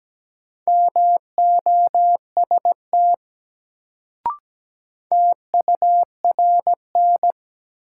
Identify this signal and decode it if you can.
Morse code, heard as a keyed tone, about 17 wpm, 700 Hz. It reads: MOST TURN
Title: Sentences from Top 200 Words - New Word Drill - 17wpm